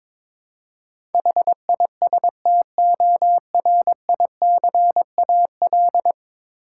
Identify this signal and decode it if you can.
Morse code, heard as a keyed tone, about 22 wpm, 700 Hz. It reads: HISTORICAL
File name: practice / 22wpm / 10 Letter Words - First 3 Letters Given - 22wpm